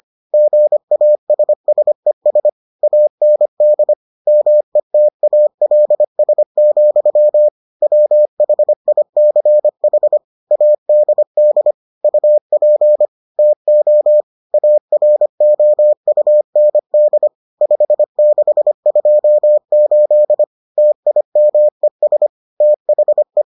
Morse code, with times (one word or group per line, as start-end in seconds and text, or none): 0.3-2.5 GASSES
2.8-3.9 AND
4.3-7.5 METALS,
7.8-10.2 WHICH
10.5-11.7 ADD
12.0-13.1 UP
13.4-14.2 TO
14.5-17.3 AROUND
17.6-20.4 5628
20.8-22.3 TIMES
22.6-23.4 THE